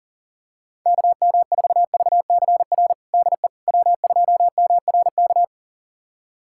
Morse code, heard as a keyed tone, about 40 wpm, 700 Hz. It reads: KM4VCR DE W2MRK
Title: USA Callsign DE Callsign 40wpm